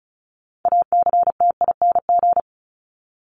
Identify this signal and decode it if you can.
Morse code, heard as a keyed tone, about 34 wpm, 700 Hz. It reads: ACTING
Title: ING Suffix - Top 100 Words with Base Words 35wpm